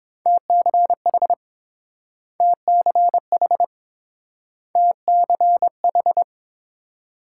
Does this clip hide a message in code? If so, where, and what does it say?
Yes: Morse code from 0.3 to 6.2 s, heard as a keyed tone, about 28 wpm, 700 Hz. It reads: TCH TCH TCH